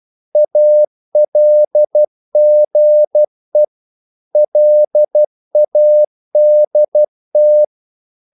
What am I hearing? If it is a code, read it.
Morse code, heard as a keyed tone, about 12 wpm, 600 Hz. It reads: ALGE LADT
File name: Korte ord DK 12wpm